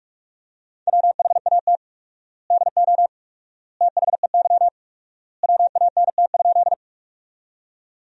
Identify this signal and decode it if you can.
Morse code, heard as a keyed tone, about 43 wpm, 700 Hz. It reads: WHAT DO THEY WANT?
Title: 4x Word-Spacing - Sentences from Top 200 Words - 45wpm